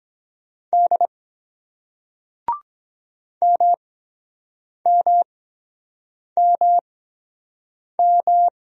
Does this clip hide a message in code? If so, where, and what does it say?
Yes: Morse code from 0.7 to 8.5 s, heard as a keyed tone, about 22 wpm, spaced as for 7 wpm, 700 Hz. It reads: D MMMM